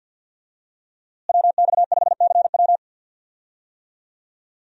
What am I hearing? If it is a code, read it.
Morse code, heard as a keyed tone, about 49 wpm, 700 Hz. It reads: WX5CW
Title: USA Call Signs - 5 Characters - Repeated 2x - 50wpm